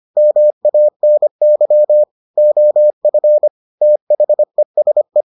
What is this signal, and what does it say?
Morse code, heard as a keyed tone, about 25 wpm, 600 Hz. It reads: MANY OF THESE